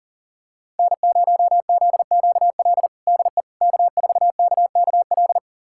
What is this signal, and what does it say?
Morse code, heard as a keyed tone, about 37 wpm, 700 Hz. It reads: N0ZQL DE K4KKL
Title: USA Callsign DE Callsign - 5 Characters - 40wpm